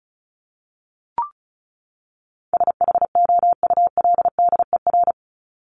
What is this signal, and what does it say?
Morse code, heard as a keyed tone, about 34 wpm, 700 Hz. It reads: SHOULDER
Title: Top 1501-1750 Words 35wpm